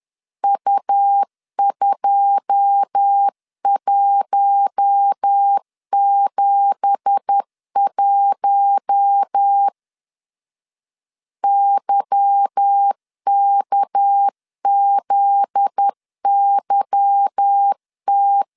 Morse code, heard as a keyed tone, about 10 wpm, 800 Hz. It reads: U2171 YKZYT
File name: CW / DET / 1 9 1500C S B 10wpm 4439